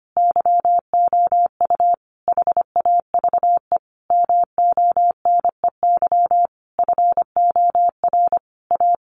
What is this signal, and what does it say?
Morse code, heard as a keyed tone, about 25 wpm, 700 Hz. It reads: YOU HAVE MONEY FOR A
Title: Sentences from Top 1400 Words 25wpm